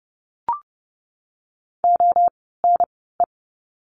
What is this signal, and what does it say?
Morse code, heard as a keyed tone, about 30 wpm, spaced as for 17 wpm, 700 Hz. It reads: ONE
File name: Lesson 16f - Letter c - Words 30wpm